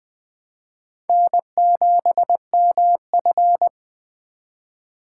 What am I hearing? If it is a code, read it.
Morse code, heard as a keyed tone, about 20 wpm, 700 Hz. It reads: N7MF
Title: Four Letter-Number Combinations - 20wpm